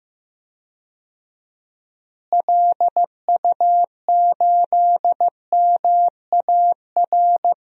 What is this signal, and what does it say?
Morse code, heard as a keyed tone, about 15 wpm, 700 Hz. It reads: LU8MAR
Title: DXCC-100 Call Signs - 6 Characters - Repeated 2x - 15wpm